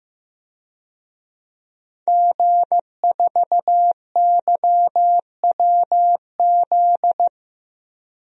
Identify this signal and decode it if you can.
Morse code, heard as a keyed tone, about 15 wpm, 700 Hz. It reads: G4YWZ
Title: DXCC-100 Call Signs - 5 Characters - 15wpm